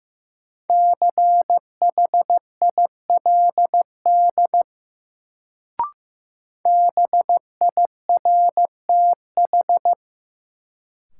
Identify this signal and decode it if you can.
Morse code, heard as a keyed tone, about 15 wpm, 700 Hz. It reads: CHILD BIRTH
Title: Sentences from Top 800 Words - New Word Drill - 15wpm